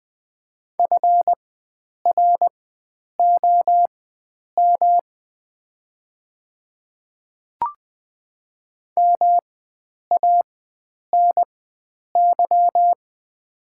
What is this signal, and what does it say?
Morse code, heard as a keyed tone, about 20 wpm, spaced as for 9 wpm, 700 Hz. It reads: FROM MANY
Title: 4x Character Spacing - Top 100 Words - 20wpm